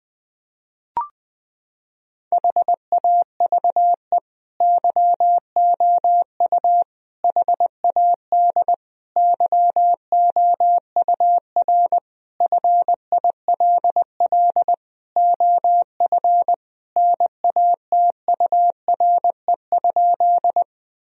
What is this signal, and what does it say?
Morse code, heard as a keyed tone, about 20 wpm, 700 Hz. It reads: HAVE YOU HAD YOUR FILL OF NATURE?
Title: Sentences from Top 1000 Words - Review Part 3 - 20wpm